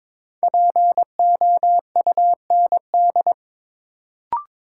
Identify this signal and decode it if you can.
Morse code, heard as a keyed tone, about 22 wpm, 700 Hz. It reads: POUND
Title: Sentences from Top 1000 Words - Review Part 2 - New Word Drill - 22wpm